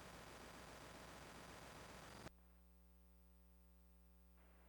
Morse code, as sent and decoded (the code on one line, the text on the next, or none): none